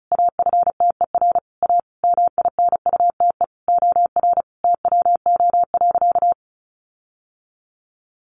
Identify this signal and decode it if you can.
Morse code, heard as a keyed tone, about 34 wpm, 700 Hz. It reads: AFTER A MINUTE OR TWO.
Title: Sentences from Top 1400 Words 35wpm